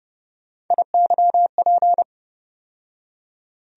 Morse code, heard as a keyed tone, about 30 wpm, 700 Hz. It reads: IYP
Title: USA Call Sign Postfixes - 3 Letters 30wpm